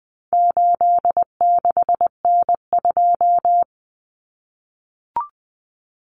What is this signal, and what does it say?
Morse code, heard as a keyed tone, about 20 wpm, 700 Hz. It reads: 86N2
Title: Common Cut Numbers - 4-Digit 20wpm